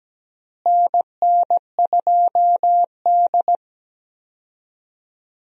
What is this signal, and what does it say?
Morse code, heard as a keyed tone, about 17 wpm, 700 Hz. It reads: NN2D
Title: USA Call Signs - 4 Characters - Repeated 3x - 17wpm